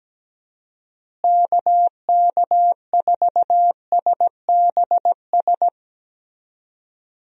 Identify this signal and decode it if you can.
Morse code, heard as a keyed tone, about 17 wpm, 700 Hz. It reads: KK4SBS
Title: USA Call Signs - 6 Characters - Repeated 3x - 17wpm